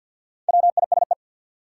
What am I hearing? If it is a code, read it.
Morse code, heard as a keyed tone, about 53 wpm, 700 Hz. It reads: WISE
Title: Sentences from Top 1600 Words - New Word Drill - 50wpm